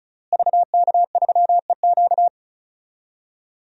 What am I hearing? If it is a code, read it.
Morse code, heard as a keyed tone, about 34 wpm, 700 Hz. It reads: VK3EQ
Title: DXCC-100 Call Signs - 5 Characters - Repeated 3x - 35wpm